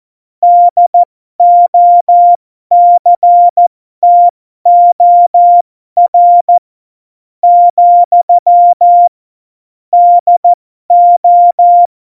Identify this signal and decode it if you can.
Morse code, heard as a keyed tone, about 14 wpm, spaced as for 11 wpm, 700 Hz. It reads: DOCTOR, DO
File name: Morsekurs / Witze